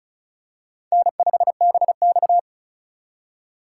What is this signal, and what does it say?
Morse code, heard as a keyed tone, about 34 wpm, 700 Hz. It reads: N5BX
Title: USA Call Signs - 4 Characters - 3x Character Spacing - 35wpm